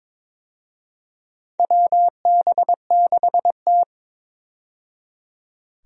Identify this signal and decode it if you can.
Morse code, heard as a keyed tone, about 22 wpm, 700 Hz. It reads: WB6T
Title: USA Call Signs - 4 Characters - 22wpm